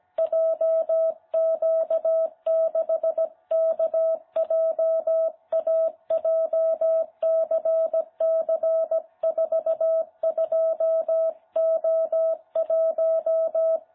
Morse code, heard as a keyed tone, about 17 wpm, 660 Hz. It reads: JQ6KJA JCC42O1
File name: jh8cbh / 2021cw